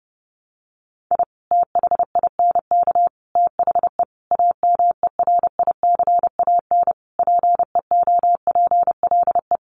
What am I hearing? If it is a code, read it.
Morse code, heard as a keyed tone, about 28 wpm, 700 Hz. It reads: I THINK THE AMERICAN PEOPLE
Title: Sets of 5 Words Encore 30wpm